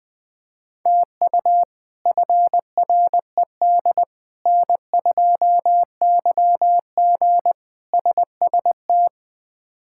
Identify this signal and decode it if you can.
Morse code, heard as a keyed tone, about 20 wpm, 700 Hz. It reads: TU FRED N2YG SST
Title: K1USN SST QSO Practice - 20wpm